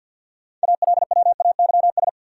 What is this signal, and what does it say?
Morse code, heard as a keyed tone, about 53 wpm, 700 Hz. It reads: ALWAYS
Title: Sentences from Top 300 Words - New Word Drill - 50wpm